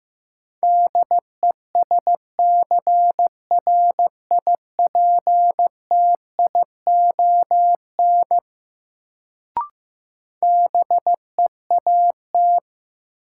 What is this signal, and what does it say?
Morse code, heard as a keyed tone, about 15 wpm, 700 Hz. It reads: DESCRIPTION BEAT
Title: Sentences from Top 1300 Words - New Word Drill Alt - 15wpm